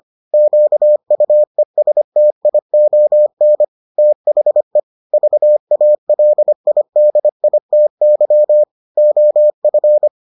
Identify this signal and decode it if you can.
Morse code, heard as a keyed tone, about 25 wpm, 600 Hz. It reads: QUESTION THE VALIDITY OF